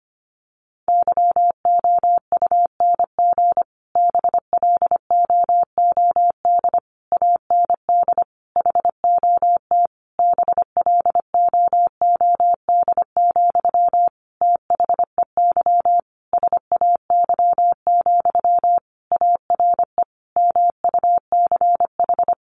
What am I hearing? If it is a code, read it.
Morse code, heard as a keyed tone, about 25 wpm, 700 Hz. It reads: YOUNG BLOOD AND HOT BLOOD, THEY SAY, ARE MUCH
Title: Sentences from Top 1100 Words 25wpm